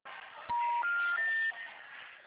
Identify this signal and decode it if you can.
Special information tone, with noise.